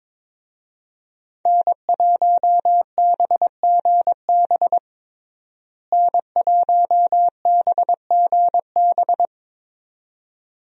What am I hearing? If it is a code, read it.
Morse code, heard as a keyed tone, about 22 wpm, 700 Hz. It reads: N1BGB N1BGB
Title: USA Call Signs - 5 Characters - Repeated 2x - 22wpm